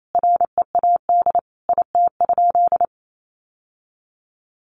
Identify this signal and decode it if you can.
Morse code, heard as a keyed tone, about 28 wpm, 700 Hz. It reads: READ IT?
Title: Sentences from Top 1200 Words 28wpm